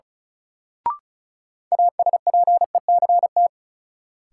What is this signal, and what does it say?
Morse code, heard as a keyed tone, about 36 wpm, 700 Hz. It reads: ASPECT
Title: Sentences from Top 1300 Words - New Word Drill - 35wpm